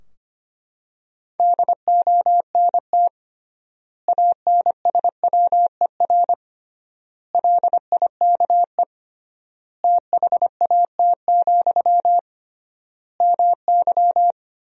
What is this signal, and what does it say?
Morse code, heard as a keyed tone, about 25 wpm, 700 Hz. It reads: DONT ANSWER LIKE THAT, MY
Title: 3x Word-Spacing - Sentences from Top 400 Words - 25wpm